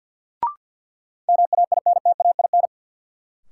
Morse code, heard as a keyed tone, about 46 wpm, 700 Hz. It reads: MAINTAIN
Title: Sentences from Top 1000 Words - Review Part 5 - New Word Drill - 50wpm